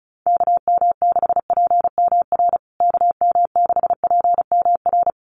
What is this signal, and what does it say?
Morse code, heard as a keyed tone, about 34 wpm, 700 Hz. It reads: KM6PMR KM6PMR
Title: USA Call Signs Repeated 2x 35wpm